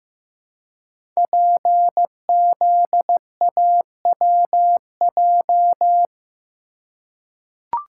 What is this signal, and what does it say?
Morse code, heard as a keyed tone, about 15 wpm, 700 Hz. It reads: PZAWJ
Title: Code Groups - 5 Letter:Numbers - 15wpm